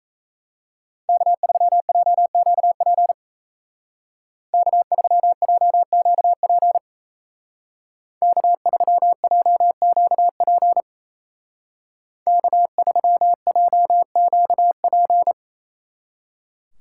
Morse code, heard as a keyed tone, about 34 wpm, 700 Hz. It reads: K3JQP K3JQP K3JQP K3JQP
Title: Speed Racing - USA 1x3 Call Signs 28wpm